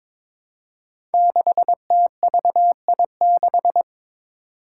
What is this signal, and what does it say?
Morse code, heard as a keyed tone, about 22 wpm, 700 Hz. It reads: 6TVI6
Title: Five Challenging Letter-Number Combinations 22wpm